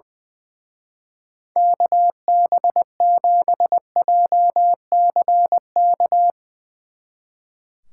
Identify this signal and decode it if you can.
Morse code, heard as a keyed tone, about 20 wpm, 700 Hz. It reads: KB7JCK
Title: Speed Racing - USA 2x3 Call Signs 20wpm